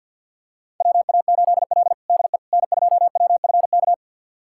Morse code, heard as a keyed tone, about 47 wpm, 700 Hz. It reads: WA8L DE N2WFK
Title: USA Callsign DE Callsign 50wpm